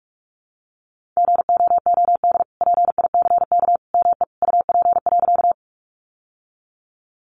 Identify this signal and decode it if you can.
Morse code, heard as a keyed tone, about 44 wpm, 700 Hz. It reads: GOOD PICK ME UP.
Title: Sentences from Top 1000 Words 45wpm